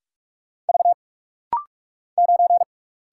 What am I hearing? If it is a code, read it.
Morse code, heard as a keyed tone, about 46 wpm, 700 Hz. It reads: V 9
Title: Single Letter-Number Alt - Rapid-Fire 45wpm